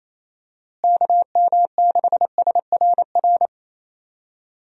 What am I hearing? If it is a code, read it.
Morse code, heard as a keyed tone, about 28 wpm, 700 Hz. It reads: KM6SRR
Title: USA Call Signs - 6 Characters - Repeated 3x - 28wpm